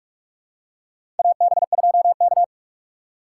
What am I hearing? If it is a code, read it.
Morse code, heard as a keyed tone, about 46 wpm, 700 Hz. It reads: AB2K
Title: USA Call Signs - 4 Characters - 4x Character Spacing - 45wpm